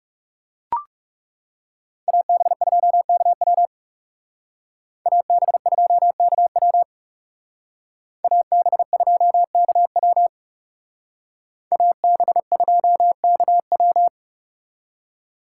Morse code, heard as a keyed tone, about 38 wpm, 700 Hz. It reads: AB2KW AB2KW AB2KW AB2KW